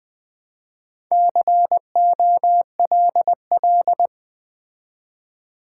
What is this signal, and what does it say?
Morse code, heard as a keyed tone, about 20 wpm, 700 Hz. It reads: COLL
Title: Build A Word - 4 Letters At A Time - 11-Letter Words - 20wpm